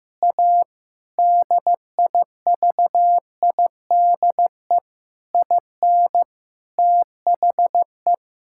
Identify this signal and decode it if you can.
Morse code, heard as a keyed tone, about 15 wpm, 700 Hz. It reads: A DIVIDE IN THE
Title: Sentences from Top 1600 Words 15wpm